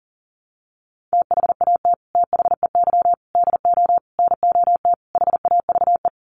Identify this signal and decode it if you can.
Morse code, heard as a keyed tone, about 40 wpm, 700 Hz. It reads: THAT THEY DO NOT HAVE
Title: Sets of 5 Words Encore2 40wpm